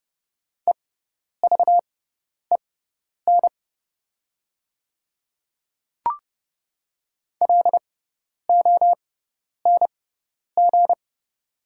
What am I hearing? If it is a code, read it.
Morse code, heard as a keyed tone, about 30 wpm, spaced as for 10 wpm, 700 Hz. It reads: EVEN LONG
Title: 6x Character Spacing - Top 300 Words - 30wpm